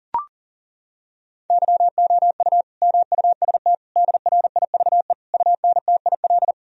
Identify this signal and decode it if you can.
Morse code, heard as a keyed tone, about 40 wpm, 700 Hz. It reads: YOU MUST DRIVE UNTIL
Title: Sentences from Top 1000 Words - Review Part 2 - 40wpm